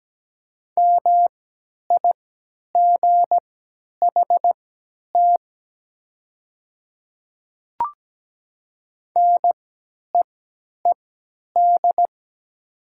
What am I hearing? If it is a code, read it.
Morse code, heard as a keyed tone, about 17 wpm, spaced as for 10 wpm, 700 Hz. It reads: MIGHT NEED
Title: 3x Character Spacing - Top 500 Words - 17wpm